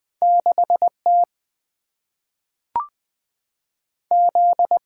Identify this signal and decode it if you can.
Morse code, heard as a keyed tone, about 20 wpm, 700 Hz. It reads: 6 T Z
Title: Single-Letter Single-Number 20wpm